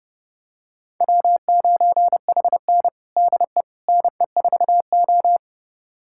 Morse code, heard as a keyed tone, about 32 wpm, 700 Hz. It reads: W9HN DE NE4O
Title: USA Callsign DE Callsign - 4 Characters - 30wpm